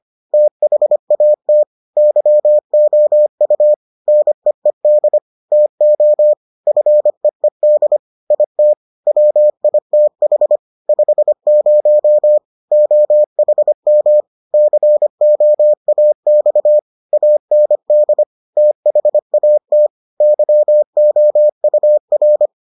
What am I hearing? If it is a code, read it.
Morse code, heard as a keyed tone, about 25 wpm, 600 Hz. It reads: THAT YOU NEED TO FEED IT WITH 50 OHM COAX AND THAT YOUR